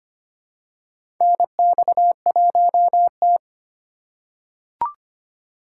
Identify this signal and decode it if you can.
Morse code, heard as a keyed tone, about 25 wpm, 700 Hz. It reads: NX1T
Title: USA Call Signs - 4 Characters - 4x Character Spacing - 25wpm